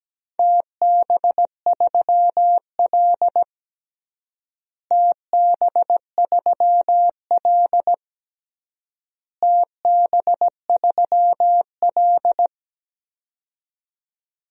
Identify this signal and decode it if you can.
Morse code, heard as a keyed tone, about 17 wpm, 700 Hz. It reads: TB3L TB3L TB3L
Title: DXCC-100 Call Signs - 4 Characters - Repeated 3x - 17wpm